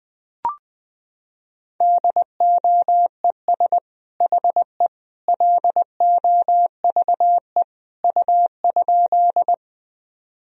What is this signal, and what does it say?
Morse code, heard as a keyed tone, about 21 wpm, 700 Hz. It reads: DOES HE LOVE U?
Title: Sentences from Top 400 Words and Abbreviations 20wpm